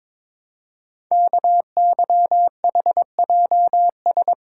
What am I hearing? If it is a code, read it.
Morse code, heard as a keyed tone, about 22 wpm, 700 Hz. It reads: KYHJS
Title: Code Groups - 5 Letters - 22wpm